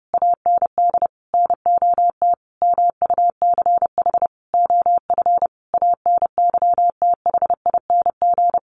Morse code, heard as a keyed tone, about 30 wpm, 700 Hz. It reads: AND NOT MUCH OF ANYTHING
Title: Sentences from Top 1500 Words 30wpm